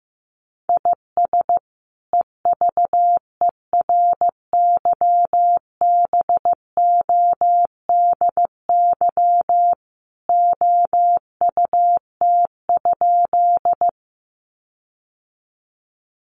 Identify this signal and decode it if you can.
Morse code, heard as a keyed tone, about 15 wpm, 700 Hz. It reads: IS EVERYBODY OUT?
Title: Sentences from Top 1400 Words 15wpm